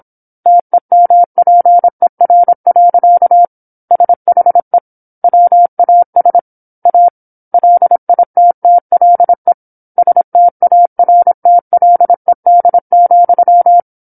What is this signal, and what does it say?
Morse code, heard as a keyed tone, about 27 wpm, 700 Hz. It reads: TEMPER. SHE WAS A LITTLE STARTLED,